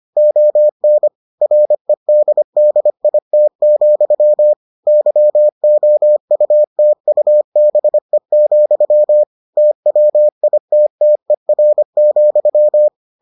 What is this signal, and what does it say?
Morse code, heard as a keyed tone, about 25 wpm, 600 Hz. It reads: ON REDDIT, YOUTUBE, TWITTER,